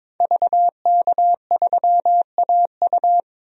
Morse code, heard as a keyed tone, about 22 wpm, 700 Hz. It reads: VK3AU
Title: DXCC-100 Call Signs - 5 Characters - 22wpm